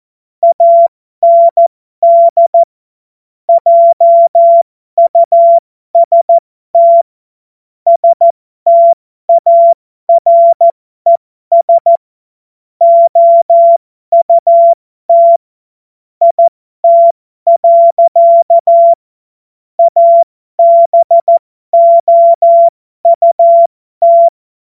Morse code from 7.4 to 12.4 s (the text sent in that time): STARES